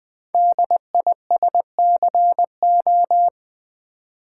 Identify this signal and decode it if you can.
Morse code, heard as a keyed tone, about 20 wpm, 700 Hz. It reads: DISCO
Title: Build A Word - 5 Letters At A Time - 10-Letter Words - 20wpm